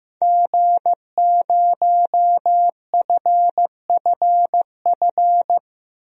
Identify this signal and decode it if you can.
Morse code, heard as a keyed tone, about 15 wpm, 700 Hz. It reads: G0FFF
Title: DXCC-100 Call Signs - 5 Characters - Repeated 3x - 15wpm